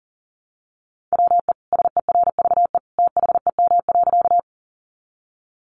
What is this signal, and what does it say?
Morse code, heard as a keyed tone, about 40 wpm, 700 Hz. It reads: WE SERVE THEM.
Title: Sentences from Top 400 Words 40wpm